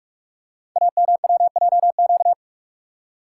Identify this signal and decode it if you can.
Morse code, heard as a keyed tone, about 46 wpm, 700 Hz. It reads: AMWJQ
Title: Code Groups - 5 Letters - 45wpm